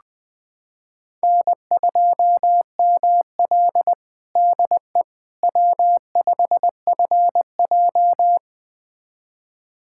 Morse code, heard as a keyed tone, about 20 wpm, 700 Hz. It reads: N2ML DE W5FJ
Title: USA Callsign DE Callsign - 4 Characters - 20wpm